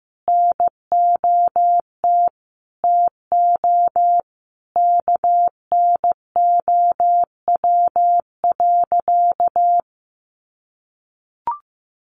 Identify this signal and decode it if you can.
Morse code, heard as a keyed tone, about 15 wpm, 700 Hz. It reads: NOT TO KNOW.
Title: Sentences from Top 600 Words 15wpm